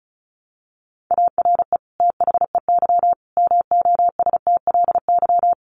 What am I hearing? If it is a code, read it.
Morse code, heard as a keyed tone, about 34 wpm, 700 Hz. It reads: ARE THEY MOSTLY
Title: Sentences from Top 1300 Words 35wpm